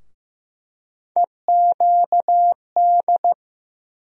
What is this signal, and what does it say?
Morse code, heard as a keyed tone, about 15 wpm, 700 Hz. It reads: EQD
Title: USA Call Sign Postfixes - 3 Letters 15wpm